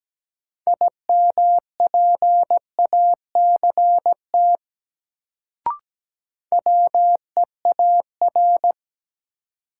Morse code, heard as a keyed tone, about 17 wpm, 700 Hz. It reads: IMPACT WEAR
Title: Sentences from Top 1000 Words - Review Part 6 - New Word Drill - 17wpm